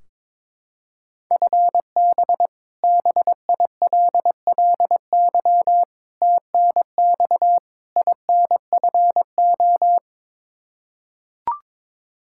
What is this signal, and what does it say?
Morse code, heard as a keyed tone, about 22 wpm, 700 Hz. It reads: FB BILLY TNX INFO
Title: K1USN SST QSO Practice 22wpm